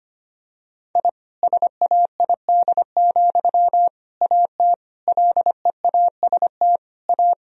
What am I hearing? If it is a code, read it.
Morse code, heard as a keyed tone, about 25 wpm, 700 Hz. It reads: I SAID, AT LEAST A
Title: Sentences from Top 1000 Words - Review Part 2 - 25wpm